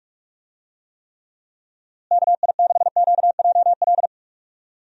Morse code, heard as a keyed tone, about 43 wpm, 700 Hz. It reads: KI6QJL
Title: USA Call Signs - 6 Characters - Repeated 2x - 45wpm